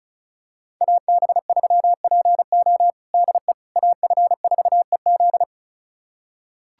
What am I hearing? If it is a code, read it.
Morse code, heard as a keyed tone, about 36 wpm, 700 Hz. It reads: AB3PO DE AF4EZ
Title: USA Callsign DE Callsign - 5 Characters - 35wpm